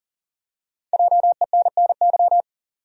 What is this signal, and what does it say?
Morse code, heard as a keyed tone, about 40 wpm, 700 Hz. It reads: JENNY
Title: USA Common Names - Max 5 Letters - 40wpm